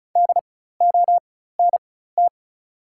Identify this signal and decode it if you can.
Morse code, heard as a keyed tone, about 36 wpm, spaced as for 16 wpm, 690 Hz. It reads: DONT
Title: 4x Character Spacing - Top 300 Words - 35wpm